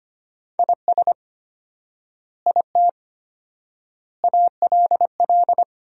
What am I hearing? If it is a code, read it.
Morse code, heard as a keyed tone, about 25 wpm, 700 Hz. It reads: IS IT ALL
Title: 4x Word-Spacing - Sentences from Top 200 Words - 25wpm